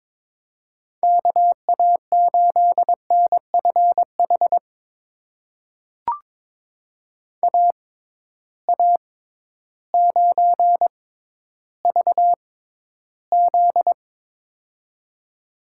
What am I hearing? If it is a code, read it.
Morse code, heard as a keyed tone, about 22 wpm, spaced as for 12 wpm, 700 Hz. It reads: KA8NFH AA9VZ
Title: USA Call Signs - 6x Character Spacing - 22wpm